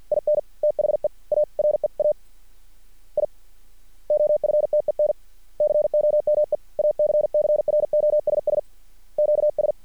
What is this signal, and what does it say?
Morse code, heard as a keyed tone, about 50 wpm, 600 Hz. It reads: IN THE AREA I OFTEN COME ACCROSS QS
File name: archived / CRSnet